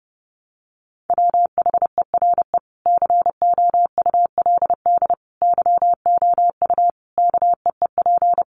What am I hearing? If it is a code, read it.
Morse code, heard as a keyed tone, about 30 wpm, 700 Hz. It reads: WHERE COULD YOU KEEP
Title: Sentences from Top 300 Words 30wpm